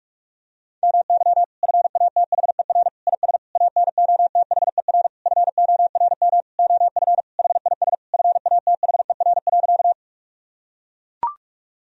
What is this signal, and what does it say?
Morse code, heard as a keyed tone, about 41 wpm, 700 Hz. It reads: MY FATHER IS ANOTHER FORM OF HIS FATHER.
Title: Sentences from Top 1000 Words - Review Part 1 - 45wpm